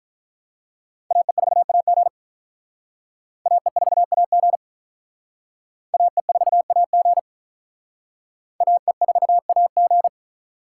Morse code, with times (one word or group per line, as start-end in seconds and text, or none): 1.1-2.1 AE4AG
3.5-4.6 AE4AG
5.9-7.2 AE4AG
8.6-10.1 AE4AG